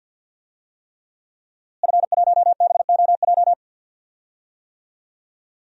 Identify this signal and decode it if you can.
Morse code, heard as a keyed tone, about 52 wpm, 700 Hz. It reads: F1BOJ
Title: DXCC-100 Call Signs - 5 Characters - Repeated 3x - 50wpm